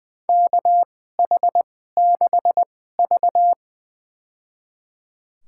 Morse code, heard as a keyed tone, about 21 wpm, spaced as for 15 wpm, 700 Hz. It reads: KH6V
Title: USA Call Signs - 4 Characters - 2x Character Spacing - 20wpm